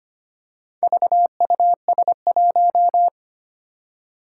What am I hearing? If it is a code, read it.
Morse code, heard as a keyed tone, about 25 wpm, 700 Hz. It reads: VUS1
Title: Four Challenging Letter-Number Combinations 25wpm